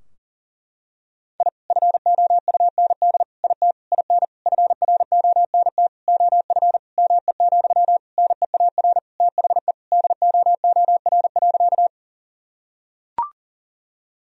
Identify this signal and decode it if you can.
Morse code, heard as a keyed tone, about 37 wpm, 700 Hz. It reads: I FOUND IT IN FRONT OF ME, NEAR THE DOOR.
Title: Sentences from Top 1000 Words - Review Part 2 - 40wpm